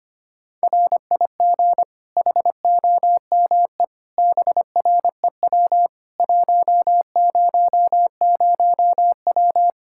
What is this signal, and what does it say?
Morse code, heard as a keyed tone, about 25 wpm, 700 Hz. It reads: RIG HOME BREW 100W
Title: Practice QSO - Exchange 2 - Info - 25wpm